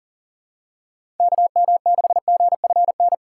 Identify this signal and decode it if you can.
Morse code, heard as a keyed tone, about 40 wpm, 700 Hz. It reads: KM6GFN
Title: USA Call Signs - 6 Characters - 2x Character Spacing - 40wpm